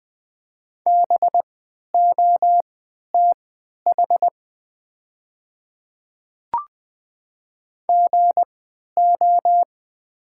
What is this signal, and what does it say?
Morse code, heard as a keyed tone, about 20 wpm, spaced as for 11 wpm, 690 Hz. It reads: BOTH GO
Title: 3x Character Spacing - Top 300 Words - 20wpm